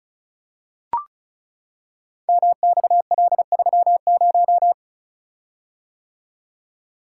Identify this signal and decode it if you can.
Morse code, heard as a keyed tone, about 36 wpm, 700 Hz. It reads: MXL30
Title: Code Groups - 5 Letter:Numbers - 35wpm